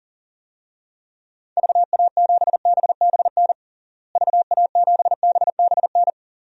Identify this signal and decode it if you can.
Morse code, heard as a keyed tone, about 40 wpm, 700 Hz. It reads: VA7BBN VA7BBN
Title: DXCC-100 Call Signs - 6 Characters - Repeated 3x - 40wpm